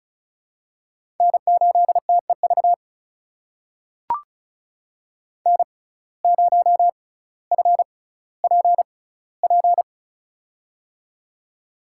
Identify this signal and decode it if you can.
Morse code, heard as a keyed tone, about 36 wpm, spaced as for 18 wpm, 700 Hz. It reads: N8TEV N0FPP
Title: USA Call Signs - 6x Character Spacing - 35wpm